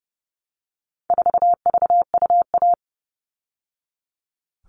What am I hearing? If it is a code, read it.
Morse code, heard as a keyed tone, about 30 wpm, 700 Hz. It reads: 4VUA
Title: All Cut Numbers - 4-Digit 30wpm